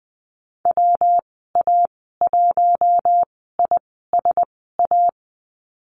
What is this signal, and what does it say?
Morse code, heard as a keyed tone, about 19 wpm, spaced as for 14 wpm, 700 Hz. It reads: WA1ISA
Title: 179 Lesson 29g - Slash - Call Signs 20wpm